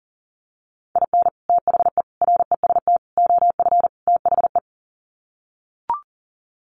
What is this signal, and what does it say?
Morse code, heard as a keyed tone, about 40 wpm, 700 Hz. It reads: IN THE REST OF THE
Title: Sets of 5 Words 40wpm